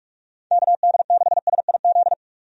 Morse code, heard as a keyed tone, about 46 wpm, 700 Hz. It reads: KD6SIZ
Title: USA Call Signs - 6 Characters - 45wpm